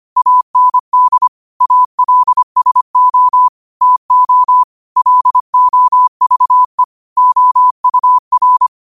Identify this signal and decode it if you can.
Morse code, heard as a keyed tone, about 25 wpm, 1000 Hz. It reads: AND ALSO TO LOVE OUR
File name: qotdfiles / 25wpm